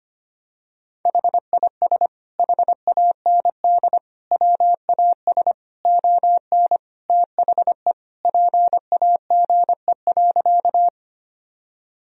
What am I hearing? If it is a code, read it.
Morse code, heard as a keyed tone, about 25 wpm, 700 Hz. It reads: HIS HAND WAS ON THE PAGE.
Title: Sentences from Top 1000 Words - Review Part 1 - 25wpm